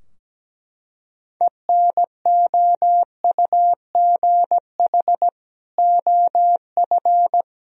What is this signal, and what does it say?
Morse code, heard as a keyed tone, about 17 wpm, 700 Hz. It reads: ENOUGH OF
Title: Sentences from Top 1000 Words - Review Part 2 - 17wpm